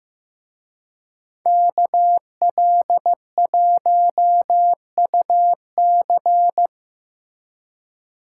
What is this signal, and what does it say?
Morse code, heard as a keyed tone, about 18 wpm, 700 Hz. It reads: KL1UC
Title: Speed Racing - USA 2x2 Call Signs 15wpm